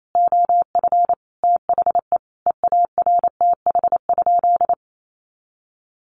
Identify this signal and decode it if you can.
Morse code, heard as a keyed tone, about 28 wpm, 700 Hz. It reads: OF THE EARTH?
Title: Sentences from Top 1200 Words 28wpm